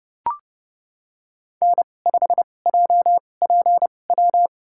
Morse code, heard as a keyed tone, about 29 wpm, spaced as for 21 wpm, 700 Hz. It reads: N5JPW
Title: USA Call Signs - 5 Characters - 2x Character Spacing - 30wpm